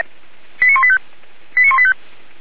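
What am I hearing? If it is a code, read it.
Five-tone selective call (ZVEI), heard as quick runs of five tones, tones of about 80 ms. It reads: 89185
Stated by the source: ZVEI five-tone selective call received by SDR, code 89134.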